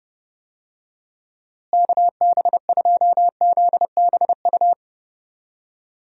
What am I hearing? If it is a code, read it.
Morse code, heard as a keyed tone, about 31 wpm, 700 Hz. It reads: KB2ZBU
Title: USA Call Signs - 6 Characters - Repeated 3x - 30wpm